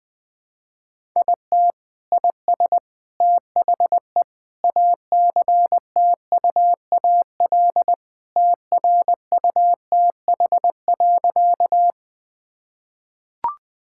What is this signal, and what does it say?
Morse code, heard as a keyed tone, about 20 wpm, 700 Hz. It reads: IT IS THE ACTUAL TRUTH.
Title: Sentences from Top 1000 Words - Review Part 5 - 20wpm